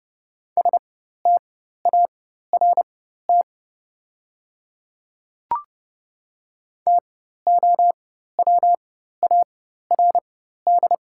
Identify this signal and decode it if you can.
Morse code, heard as a keyed tone, about 30 wpm, spaced as for 14 wpm, 700 Hz. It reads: START TOWARD
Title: 4x Character Spacing - Top 500 Words - 30wpm